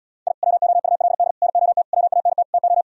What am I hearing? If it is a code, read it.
Morse code, heard as a keyed tone, about 77 wpm, 700 Hz. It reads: I WOULD NOT WANT TO